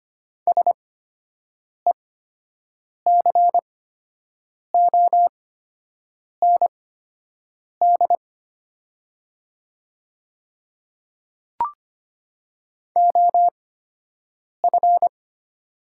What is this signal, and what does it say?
Morse code, heard as a keyed tone, about 25 wpm, spaced as for 7 wpm, 700 Hz. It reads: SECOND OF
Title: 8x Character Spacing - Top 300 Words - 25wpm